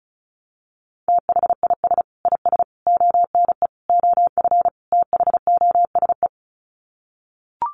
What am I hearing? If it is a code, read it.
Morse code, heard as a keyed tone, about 34 wpm, 700 Hz. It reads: THIS IS ONE OF THOSE
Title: Sets of 5 Words Encore2 35wpm